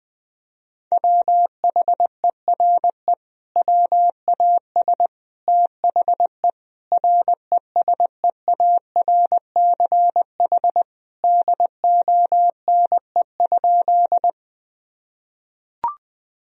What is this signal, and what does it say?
Morse code, heard as a keyed tone, about 21 wpm, 700 Hz. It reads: WHERE WAS THE RESEARCH DONE?
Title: Sentences from Top 1000 Words - Review Part 4 - 20wpm